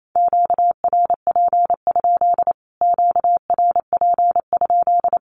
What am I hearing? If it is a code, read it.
Morse code, heard as a keyed tone, about 28 wpm, 700 Hz. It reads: QRP? QRP?
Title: Q-Codes Repeated 3x 28wpm